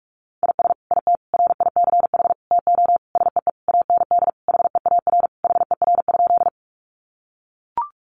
Morse code, heard as a keyed tone, about 44 wpm, 700 Hz. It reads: IS IT RICH TO SEE AND HEAR HER?
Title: Sentences from Top 1100 Words 45wpm